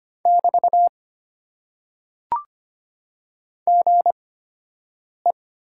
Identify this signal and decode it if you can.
Morse code, heard as a keyed tone, about 25 wpm, spaced as for 7 wpm, 700 Hz. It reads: <BT> GE
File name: Top 90 QSO Elements - 8x Character Spacing - 25wpm